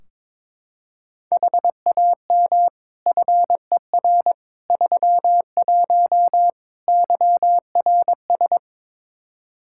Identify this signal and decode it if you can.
Morse code, heard as a keyed tone, about 22 wpm, 700 Hz. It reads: HAM FER 31 YRS
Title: Practice QSO - Exchange 3 - More Info - 22wpm